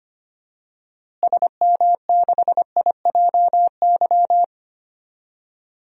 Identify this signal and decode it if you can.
Morse code, heard as a keyed tone, about 25 wpm, 700 Hz. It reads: SM6IJY
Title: DXCC-100 Call Signs - 6 Characters - Repeated 2x - 25wpm